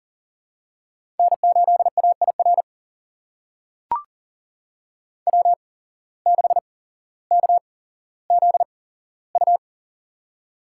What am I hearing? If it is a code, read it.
Morse code, heard as a keyed tone, about 40 wpm, spaced as for 17 wpm, 700 Hz. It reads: N8AIR W6KZU
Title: USA Call Signs - 5 Characters - 8x Character Spacing - 40wpm